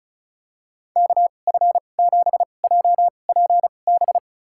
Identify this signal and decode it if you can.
Morse code, heard as a keyed tone, about 34 wpm, spaced as for 25 wpm, 700 Hz. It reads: KF7JPB
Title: USA Call Signs - 6 Characters - 2x Character Spacing - 35wpm